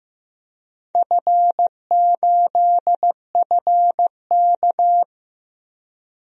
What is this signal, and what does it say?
Morse code, heard as a keyed tone, about 15 wpm, 700 Hz. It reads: F8FK
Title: DXCC-100 Call Signs - 4 Characters - Repeated 2x - 15wpm